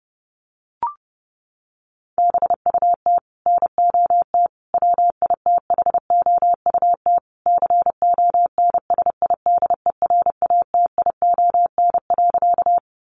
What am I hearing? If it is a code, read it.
Morse code, heard as a keyed tone, about 30 wpm, 700 Hz. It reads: BUT NOT WITHOUT CONSIDERATION.
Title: Sentences from Top 1500 Words 30wpm